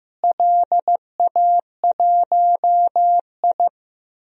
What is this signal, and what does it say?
Morse code, heard as a keyed tone, about 15 wpm, 700 Hz. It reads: LA1I
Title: DXCC-100 Call Signs - 4 Characters - Repeated 3x - 15wpm